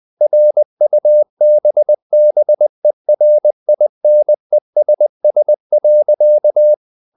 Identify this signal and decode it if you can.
Morse code, heard as a keyed tone, about 20 wpm, 600 Hz. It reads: RUBBERINESS.
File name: the old man and the sea part 2 20wpm